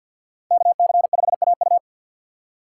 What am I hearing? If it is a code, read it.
Morse code, heard as a keyed tone, about 51 wpm, 700 Hz. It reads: KC5AU
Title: USA Call Signs - 3x Character Spacing - 50wpm